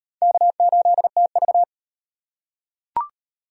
Morse code, heard as a keyed tone, about 38 wpm, 700 Hz. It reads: K8TV